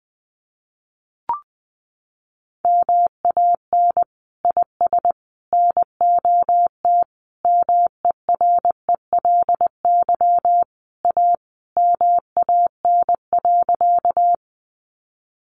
Morse code, heard as a keyed tone, about 20 wpm, 700 Hz. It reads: MAN IS NOT MERELY A MAN.
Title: Sentences from Top 1400 Words 20wpm